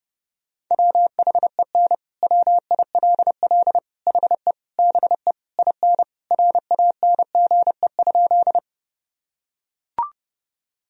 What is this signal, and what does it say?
Morse code, heard as a keyed tone, about 30 wpm, 700 Hz. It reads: WHEN WILL HE BE IN RANGE?
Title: Sentences from Top 1000 Words - Review Part 4 - 30wpm